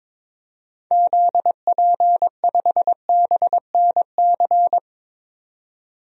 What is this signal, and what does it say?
Morse code, heard as a keyed tone, about 22 wpm, 700 Hz. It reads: ZP5BNC
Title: DXCC-100 Call Signs - 6 Characters - 22wpm